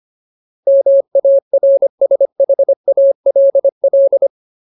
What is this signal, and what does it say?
Morse code, heard as a keyed tone, about 25 wpm, 550 Hz. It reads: MARSHALL
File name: Ultimate Word List / 25WPM Ultimate